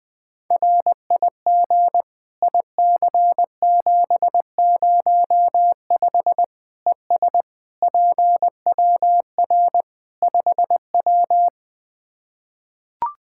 Morse code, heard as a keyed tone, about 21 wpm, 700 Hz. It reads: RIG IC705 ES PWR 5W
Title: Practice QSO - Exchange 2 - Info - 20wpm